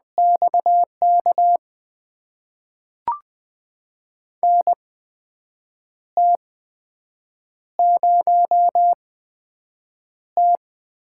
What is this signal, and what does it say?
Morse code, heard as a keyed tone, about 20 wpm, spaced as for 7 wpm, 700 Hz. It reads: XK NT0T
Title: USA Call Signs - 8x Character Spacing - 20wpm